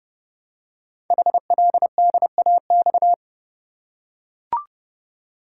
Morse code, heard as a keyed tone, about 30 wpm, 700 Hz. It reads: HLDAX
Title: Code Groups - 5 Letters - 30wpm